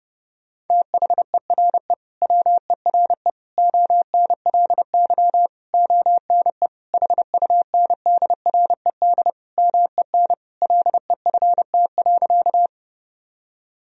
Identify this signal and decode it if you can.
Morse code, heard as a keyed tone, about 30 wpm, 700 Hz. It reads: THERE WERE ONLY ONE HUNDRED MEN LEFT.
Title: Sentences from Top 1000 Words - Review Part 2 - 30wpm